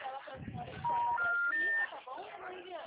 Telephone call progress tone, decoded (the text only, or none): special information tone